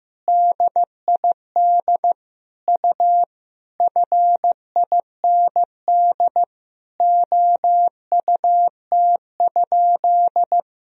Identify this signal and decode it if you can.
Morse code, heard as a keyed tone, about 15 wpm, 700 Hz. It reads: DID U FIND OUT?
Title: Sentences from Top 200 Words and Abbreviations 15wpm